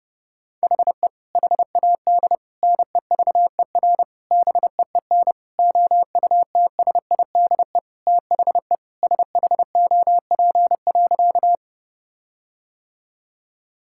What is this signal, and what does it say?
Morse code, heard as a keyed tone, about 30 wpm, 700 Hz. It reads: HE HAD NEVER BEEN OUTSIDE THE SHOP.
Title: Sentences from Top 1600 Words 30wpm